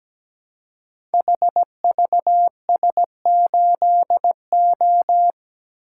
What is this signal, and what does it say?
Morse code, heard as a keyed tone, about 17 wpm, 700 Hz. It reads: HVS8O
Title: Code Groups - 5 Letter:Numbers - 17wpm